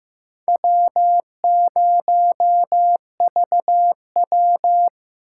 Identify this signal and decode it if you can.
Morse code, heard as a keyed tone, about 15 wpm, 700 Hz. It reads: W0VW
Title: USA Call Signs - 2x Character Spacing - 15wpm